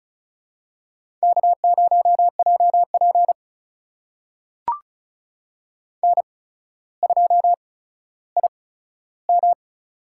Morse code, heard as a keyed tone, about 36 wpm, spaced as for 15 wpm, 700 Hz. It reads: K0JP N2IM
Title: USA Call Signs - 8x Character Spacing - 35wpm